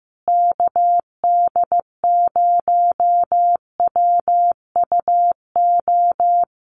Morse code, heard as a keyed tone, about 15 wpm, 700 Hz. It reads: KD0WUO
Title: USA Call Signs Repeated 3x 15wpm